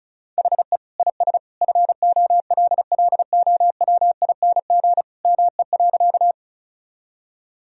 Morse code, heard as a keyed tone, about 36 wpm, 700 Hz. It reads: HE IS FOLLOWING ME.
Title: Sentences from Top 1000 Words - Review Part 4 - 35wpm